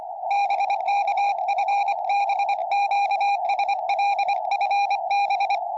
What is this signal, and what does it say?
Morse code, heard as a keyed tone, about 25 wpm, 800 Hz, with radio noise. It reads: BK FB QSL FB
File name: tropo / sm6net 70cm